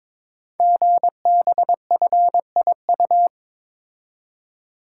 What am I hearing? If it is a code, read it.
Morse code, heard as a keyed tone, about 22 wpm, 700 Hz. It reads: GBFIU
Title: Code Groups - 5 Letters - 22wpm